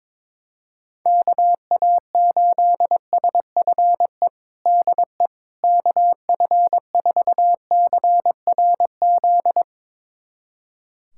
Morse code, heard as a keyed tone, about 22 wpm, 700 Hz. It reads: KA8SFE DE KF4CRZ
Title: USA Callsign DE Callsign - 6 Characters - 22wpm